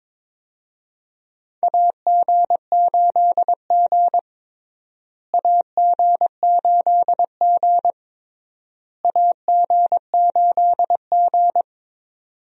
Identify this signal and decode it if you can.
Morse code, heard as a keyed tone, about 22 wpm, 700 Hz. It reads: AG8G AG8G AG8G
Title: USA Call Signs - 4 Characters - Repeated 3x - 22wpm